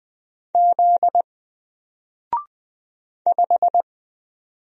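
Morse code, heard as a keyed tone, about 20 wpm, 700 Hz. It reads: Z 5
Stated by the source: Single letter and number Morse drill at 20 wpm, from N4LCD.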